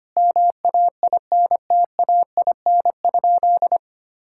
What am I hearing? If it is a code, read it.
Morse code, heard as a keyed tone, about 25 wpm, 700 Hz. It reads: MAINTAIN?
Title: Sentences from Top 1000 Words - Review Part 5 - 25wpm